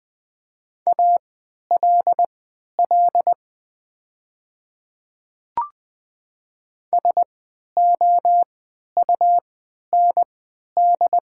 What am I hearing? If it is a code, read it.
Morse code, heard as a keyed tone, about 20 wpm, spaced as for 11 wpm, 700 Hz. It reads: ALL SOUND
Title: 3x Character Spacing - Top 100 Words - 20wpm